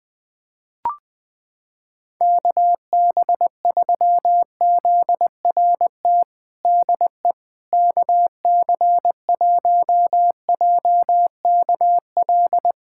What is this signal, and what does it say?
Morse code, heard as a keyed tone, about 20 wpm, 700 Hz. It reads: KB3ZRT DE KC1JKL
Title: USA Callsign DE Callsign 20wpm